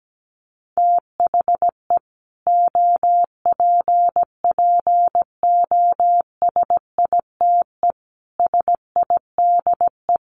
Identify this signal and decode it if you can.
Morse code, heard as a keyed tone, about 17 wpm, 700 Hz. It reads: THE OPPOSITE SIDE
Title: Sets of 5 Words Encore 17wpm